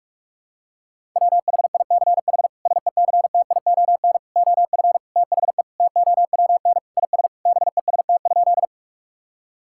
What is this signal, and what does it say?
Morse code, heard as a keyed tone, about 46 wpm, 700 Hz. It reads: WHICH SECTION OF THE TOWN IS BEST?
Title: Sentences from Top 1000 Words - Review Part 5 - 45wpm